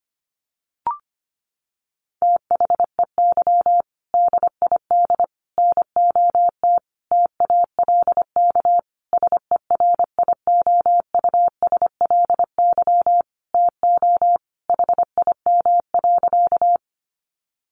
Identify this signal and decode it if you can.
Morse code, heard as a keyed tone, about 25 wpm, 700 Hz. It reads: THEY DID NOT TALK SERIOUSLY TO HIM.
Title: Sentences from Top 1200 Words 25wpm